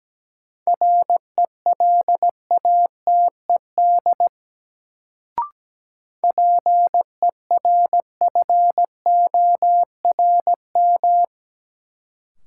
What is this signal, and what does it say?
Morse code, heard as a keyed tone, about 17 wpm, 700 Hz. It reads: RELATED PERFORM
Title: Sentences from Top 1000 Words - Review Part 5 - New Word Drill - 17wpm